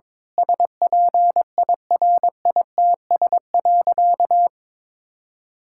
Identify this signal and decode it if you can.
Morse code, heard as a keyed tone, about 21 wpm, 700 Hz. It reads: SPIRITS.
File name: Difficult Sentences with Hints - 22wpm